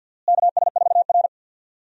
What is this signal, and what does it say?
Morse code, heard as a keyed tone, about 51 wpm, 700 Hz. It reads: KS4R
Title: USA Call Signs - 4 Characters - Repeated 3x - 50wpm